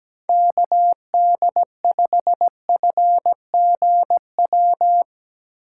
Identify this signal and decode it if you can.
Morse code, heard as a keyed tone, about 17 wpm, 700 Hz. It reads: KD5FGW
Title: USA Call Signs - 6 Characters - 2x Character Spacing - 17wpm